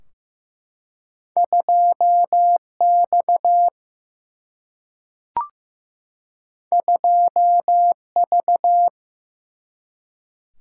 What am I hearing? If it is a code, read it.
Morse code, heard as a keyed tone, about 15 wpm, 700 Hz. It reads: 2X 2V
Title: Single-Letter Single-Number 15wpm